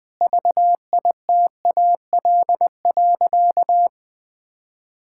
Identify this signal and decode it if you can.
Morse code, heard as a keyed tone, about 21 wpm, 700 Hz. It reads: VITAL.
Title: Sentences with 8-Word Phrases - Spoken Beginning - 20wpm